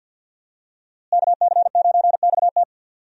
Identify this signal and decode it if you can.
Morse code, heard as a keyed tone, about 52 wpm, 700 Hz. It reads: KC9XT
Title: USA Call Signs - 5 Characters - Repeated 2x - 50wpm